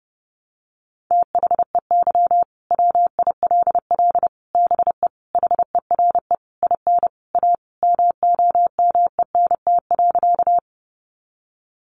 Morse code, heard as a keyed tone, about 30 wpm, 700 Hz. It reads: THEY WILL BE HERE IN A MOMENT.
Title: Sentences from Top 1000 Words 30wpm